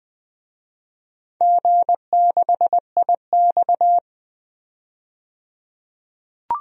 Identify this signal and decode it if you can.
Morse code, heard as a keyed tone, about 21 wpm, 700 Hz. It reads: G6IX
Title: DXCC-100 Call Signs - 4 Characters - 20wpm